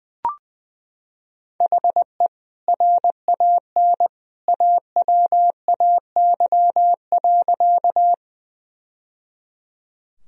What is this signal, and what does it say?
Morse code, heard as a keyed tone, about 20 wpm, 700 Hz. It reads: HE RAN AWAY.
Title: Sentences from Top 1000 Words - Review Part 3 - 20wpm